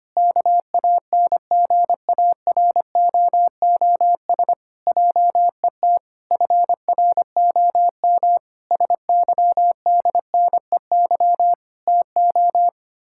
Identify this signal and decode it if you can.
Morse code, heard as a keyed tone, about 25 wpm, 700 Hz. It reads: KANGAROOS JET FROM SYDNEY TO
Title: Pangram Challenge v2 - 25wpm